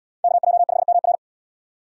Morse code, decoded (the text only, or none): KJ5MR